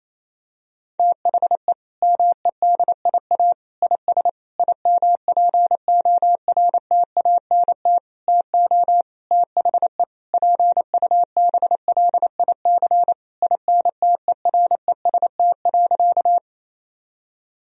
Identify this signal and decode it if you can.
Morse code, heard as a keyed tone, about 28 wpm, 700 Hz. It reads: THE MEDIA IS IMPORTANT TO THE PUBLIC INTEREST.
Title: Sentences from Top 1000 Words - Review Part 4 - 28wpm